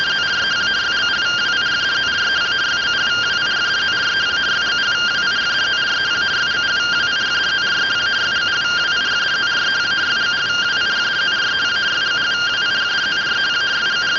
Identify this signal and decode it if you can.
Two-tone teleprinter signal, about 170 Hz shift.